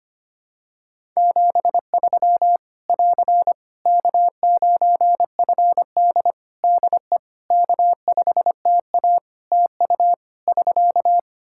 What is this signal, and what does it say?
Morse code, heard as a keyed tone, about 25 wpm, 700 Hz. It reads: 73 <AR> K9FD DE K5TA TU <SK>
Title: K1USN SST QSO Practice 25wpm